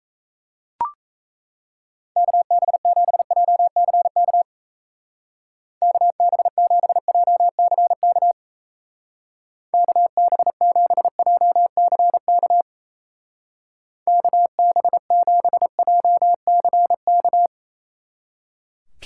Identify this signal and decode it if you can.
Morse code, heard as a keyed tone, about 35 wpm, 700 Hz. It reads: KB7JCK KB7JCK KB7JCK KB7JCK
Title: Speed Racing - USA 2x3 Call Signs 28wpm